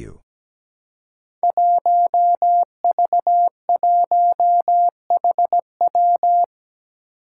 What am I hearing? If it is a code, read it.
Morse code, heard as a keyed tone, about 17 wpm, 700 Hz. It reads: 1V1HW
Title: Five Challenging Letter-Number Combinations 17wpm